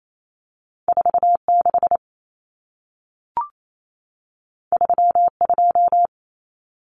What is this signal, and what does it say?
Morse code, heard as a keyed tone, about 28 wpm, 700 Hz. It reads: 46 32
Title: Numbers - 2-Digit 28wpm